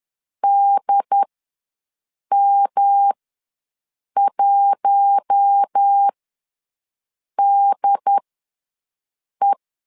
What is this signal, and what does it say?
Morse code, heard as a keyed tone, about 10 wpm, spaced as for 5 wpm, 800 Hz. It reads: DM1DE
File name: LEZIONI / 6 PKX0 / 6 PK ws 10wpm 3745